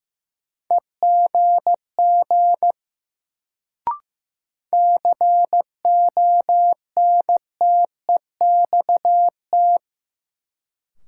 Morse code, heard as a keyed tone, about 15 wpm, 700 Hz. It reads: EGG CONTEXT
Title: Sentences from Top 1000 Words - Review Part 7 - New Word Drill - 15wpm